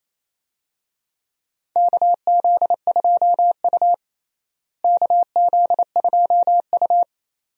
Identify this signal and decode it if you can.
Morse code, heard as a keyed tone, about 28 wpm, 700 Hz. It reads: KZ2U KZ2U
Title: USA Call Signs - 4 Characters - Repeated 2x - 28wpm